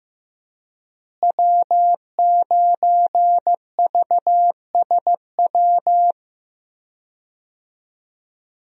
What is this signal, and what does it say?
Morse code, heard as a keyed tone, about 16 wpm, 700 Hz. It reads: W9VSW
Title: USA Call Signs - 5 Characters - Repeated 3x - 15wpm